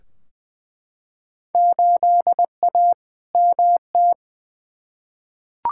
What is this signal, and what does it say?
Morse code, heard as a keyed tone, about 21 wpm, 700 Hz. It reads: 8A MT
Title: Field Day Exchanges Encore - 20wpm